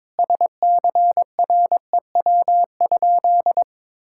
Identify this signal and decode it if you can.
Morse code, heard as a keyed tone, about 22 wpm, 700 Hz. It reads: SCREW?
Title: Sentences from Top 1600 Words 22wpm